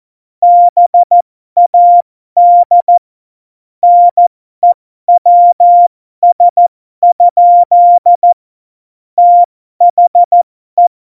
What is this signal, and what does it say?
Morse code, heard as a keyed tone, about 14 wpm, spaced as for 12 wpm, 700 Hz. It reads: BAD NEWS? THE